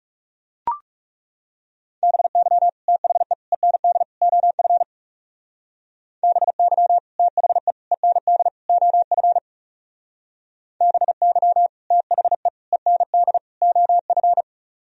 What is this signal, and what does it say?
Morse code, heard as a keyed tone, about 35 wpm, 700 Hz. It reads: BY THE END OF BY THE END OF BY THE END OF
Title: Speed-Racing - Sets of 4 Words 30wpm